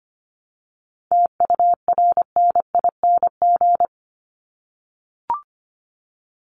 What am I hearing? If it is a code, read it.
Morse code, heard as a keyed tone, about 25 wpm, 700 Hz. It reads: TURNING
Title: ING Suffix - Top 100 Words with Base Words 25wpm